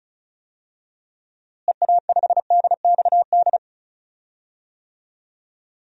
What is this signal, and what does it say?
Morse code, heard as a keyed tone, about 34 wpm, 700 Hz. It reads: EA5DXD
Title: DXCC-100 Call Signs - 6 Characters - Repeated 3x - 35wpm